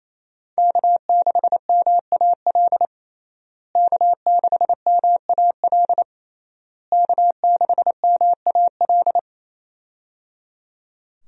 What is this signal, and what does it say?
Morse code, heard as a keyed tone, about 28 wpm, 700 Hz. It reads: K6MAL K6MAL K6MAL
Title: USA Call Signs - 5 Characters - Repeated 3x - 28wpm